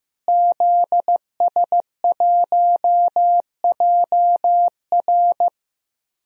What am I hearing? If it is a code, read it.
Morse code, heard as a keyed tone, about 15 wpm, 700 Hz. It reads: ZS1JR
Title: DXCC-100 Call Signs - 5 Characters - Repeated 3x - 15wpm